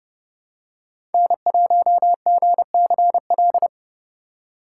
Morse code, heard as a keyed tone, about 30 wpm, 700 Hz. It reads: N1GCL
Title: USA Call Signs - 5 Characters - 2x Character Spacing - 30wpm